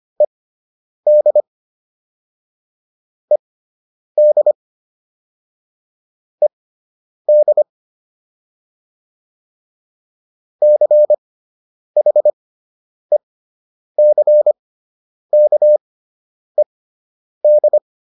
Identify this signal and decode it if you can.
Morse code, heard as a keyed tone, about 25 wpm, spaced as for 9 wpm, 600 Hz. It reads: ED ED ED CHECKED